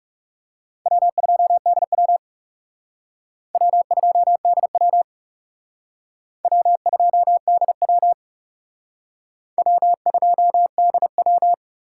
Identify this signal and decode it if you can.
Morse code, heard as a keyed tone, about 38 wpm, 700 Hz. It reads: W2DW W2DW W2DW W2DW
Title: Speed Racing - USA 1x2 Call Signs 30wpm